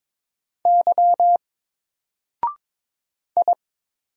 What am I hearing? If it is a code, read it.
Morse code, heard as a keyed tone, about 22 wpm, 700 Hz. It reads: Y I
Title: Single Character (Letter:Number:Punctuation:Prosign) Alt 22wpm